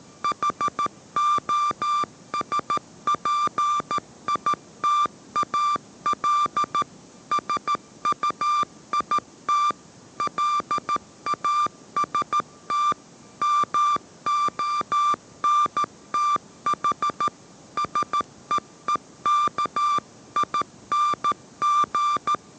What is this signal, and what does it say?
Morse code, heard as a keyed tone, about 13 wpm, 1180 Hz. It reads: HOSPITAL SUIT LAST MONTH SEEKING